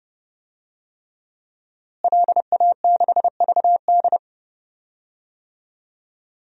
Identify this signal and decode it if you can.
Morse code, heard as a keyed tone, about 30 wpm, 700 Hz. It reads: LA6VD
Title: DXCC-100 Call Signs - 5 Characters - Repeated 3x - 30wpm